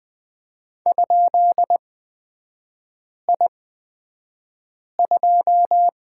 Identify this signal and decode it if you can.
Morse code, heard as a keyed tone, about 20 wpm, spaced as for 5 wpm, 700 Hz. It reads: ?I2
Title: 236 Lesson 38e - Letter q - Three Characters 20wpm